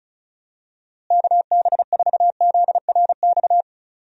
Morse code, heard as a keyed tone, about 36 wpm, 700 Hz. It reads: KB4ZRX
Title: USA Call Signs - 6 Characters - Repeated 2x - 35wpm